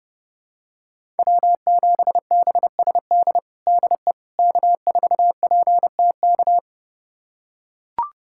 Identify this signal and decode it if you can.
Morse code, heard as a keyed tone, about 30 wpm, 700 Hz. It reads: W7BSD DE K4PTK
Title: USA Callsign DE Callsign - 5 Characters - 30wpm